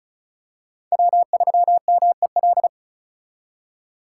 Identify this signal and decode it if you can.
Morse code, heard as a keyed tone, about 36 wpm, 700 Hz. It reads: W3MEL
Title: K1USN SST QSO Practice - 35wpm